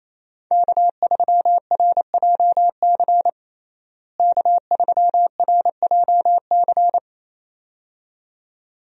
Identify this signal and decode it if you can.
Morse code, heard as a keyed tone, about 28 wpm, 700 Hz. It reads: K3RJC K3RJC
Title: USA Call Signs - 5 Characters - Repeated 2x - 28wpm